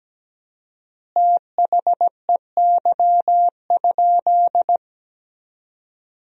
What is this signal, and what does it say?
Morse code, heard as a keyed tone, about 17 wpm, 700 Hz. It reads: THEY?
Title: 4x Word-Spacing - Sentences from Top 200 Words - 17wpm